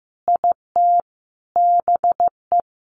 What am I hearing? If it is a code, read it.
Morse code, heard as a keyed tone, about 15 wpm, 700 Hz. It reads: IT BE
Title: Sentences from Top 1200 Words 15wpm